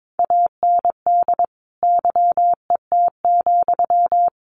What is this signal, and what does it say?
Morse code, heard as a keyed tone, about 22 wpm, 700 Hz. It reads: AND YET,
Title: Sentences from Top 1400 Words 22wpm